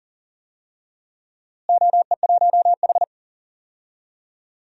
Morse code, heard as a keyed tone, about 39 wpm, 700 Hz. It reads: OE1H
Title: DXCC-100 Call Signs - 4 Characters - Repeated 3x - 40wpm